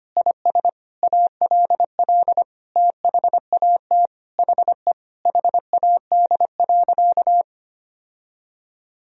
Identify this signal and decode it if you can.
Morse code, heard as a keyed tone, about 25 wpm, 700 Hz. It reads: IS ALL THAT HE HAD.
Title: Sentences from Top 1000 Words - Review Part 2 - 25wpm